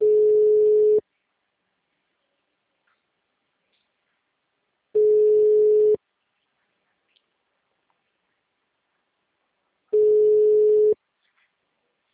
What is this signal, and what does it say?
Ringback tone (Europe, 425 Hz).
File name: out- -2007- - - .41378